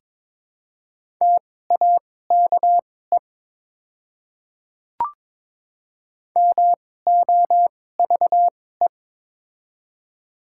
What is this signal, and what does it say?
Morse code, heard as a keyed tone, about 21 wpm, spaced as for 16 wpm, 690 Hz. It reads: TAKE MOVE
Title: 2x Character Spacing - Top 200 Words - 22wpm